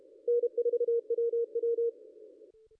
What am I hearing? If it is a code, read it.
Morse code, heard as a keyed tone, about 33 wpm, 470 Hz. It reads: N4WW
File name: contests / ss-cw-k5tr-op-n5rz -0600